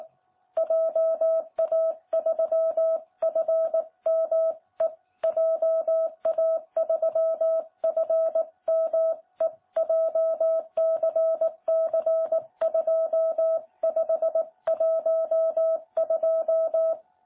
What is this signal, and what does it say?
Morse code, heard as a keyed tone, about 18 wpm, 660 Hz. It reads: JA3FME JA3FME JCC2512
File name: jh8cbh / 2021cw